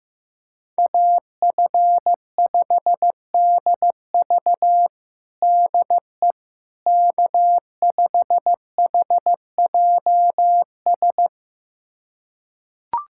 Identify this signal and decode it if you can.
Morse code, heard as a keyed tone, about 15 wpm, 700 Hz. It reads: AF5DV DE K5HJS
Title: USA Callsign DE Callsign - 5 Characters - 15wpm